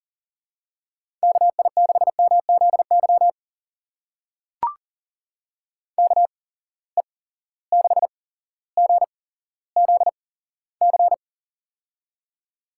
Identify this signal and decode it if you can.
Morse code, heard as a keyed tone, about 37 wpm, spaced as for 17 wpm, 700 Hz. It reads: KI6MZY KE6GZC
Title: USA Call Signs - 6 Characters - 8x Character Spacing - 40wpm